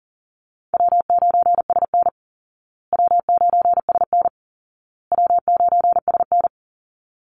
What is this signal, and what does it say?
Morse code, heard as a keyed tone, about 40 wpm, 700 Hz. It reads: W9SN W9SN W9SN
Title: Easy Call Signs Repeated 3x - 40wpm